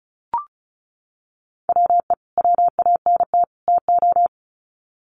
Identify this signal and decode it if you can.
Morse code, heard as a keyed tone, about 34 wpm, 700 Hz. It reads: WE WANT TO
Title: Sets of 3 Words 35wpm